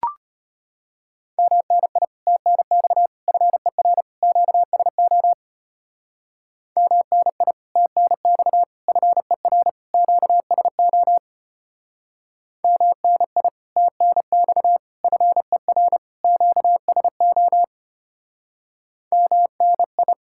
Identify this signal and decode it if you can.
Morse code, heard as a keyed tone, about 33 wpm, 700 Hz. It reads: MNI TNX FER QSO MNI TNX FER QSO MNI TNX FER QSO MNI